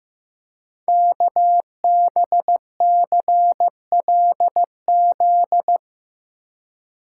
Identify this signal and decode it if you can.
Morse code, heard as a keyed tone, about 15 wpm, 700 Hz. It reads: KBCLZ
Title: Code Groups - 5 Letters - 15wpm